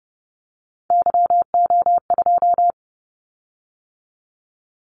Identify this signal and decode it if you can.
Morse code, heard as a keyed tone, about 30 wpm, 700 Hz. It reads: YO2
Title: Callsign Prefixes for Top 25 DXCC - 30wpm